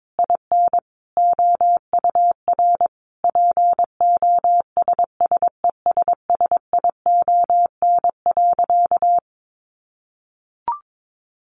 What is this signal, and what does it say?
Morse code, heard as a keyed tone, about 22 wpm, 700 Hz. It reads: IN OUR POSSESSION.
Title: Sentences from Top 1400 Words 22wpm